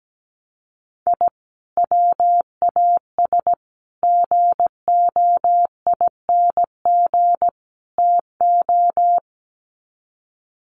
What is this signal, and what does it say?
Morse code, heard as a keyed tone, about 17 wpm, 700 Hz. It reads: I WAS GOING TO
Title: Sets of 4 Words 17wpm